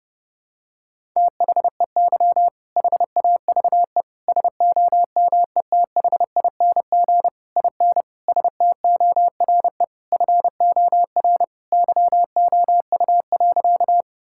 Morse code, heard as a keyed tone, about 30 wpm, 700 Hz. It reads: THEY HAVE SOMETHING IN STORE FOR YOU.
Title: Sentences from Top 1000 Words - Review Part 5 - 30wpm